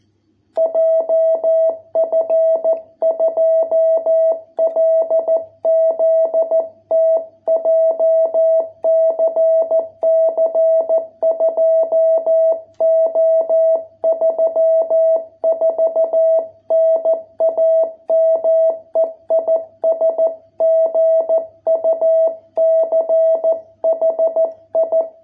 Morse code, heard as a keyed tone, about 14 wpm, 660 Hz. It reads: JF2LZT JCC 2O34 NAME IS GUCHI